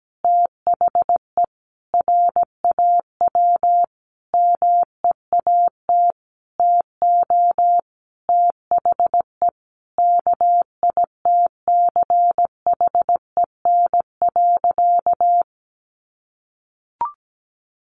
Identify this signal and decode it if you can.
Morse code, heard as a keyed tone, about 17 wpm, 700 Hz. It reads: THE RAW MEAT TO THE KITCHEN.
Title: Sentences from Top 1500 Words 17wpm